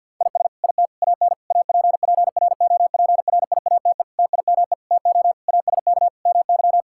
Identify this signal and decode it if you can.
Morse code, heard as a keyed tone, about 50 wpm, 700 Hz. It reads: IS IT AN APPROPRIATE TIME TO ASK MY
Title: Sentences from Top 1000 Words - Review Part 5 - 50wpm